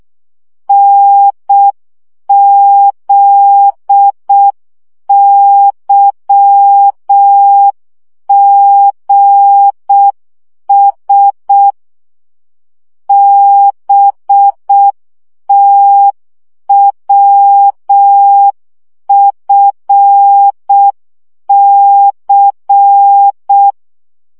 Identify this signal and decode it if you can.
Morse code, heard as a keyed tone, about 6 wpm, 800 Hz. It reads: NZYGS BTWFC
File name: morze vizsga 2011-1 5